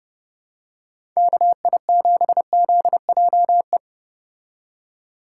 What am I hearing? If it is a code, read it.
Morse code, heard as a keyed tone, about 30 wpm, 700 Hz. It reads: KI7ZJE